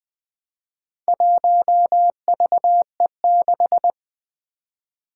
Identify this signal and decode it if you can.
Morse code, heard as a keyed tone, about 20 wpm, 700 Hz. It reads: 1VE6
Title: Four Challenging Letter-Number Combinations 20wpm